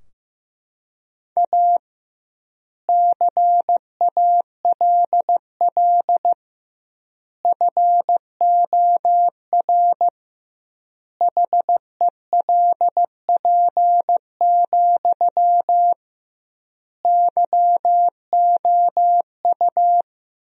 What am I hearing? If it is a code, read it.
Morse code, heard as a keyed tone, about 15 wpm, 700 Hz. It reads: A CALL FOR HELP, YOU
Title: 2x Word-Spacing - Sentences from Top 300 Words - 15wpm